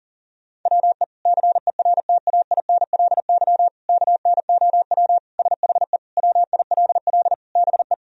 Morse code, heard as a keyed tone, about 37 wpm, 700 Hz. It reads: WE CERTAINLY KNOW SHE WILL BE
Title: Sentences from Top 1000 Words - Review Part 4 - 40wpm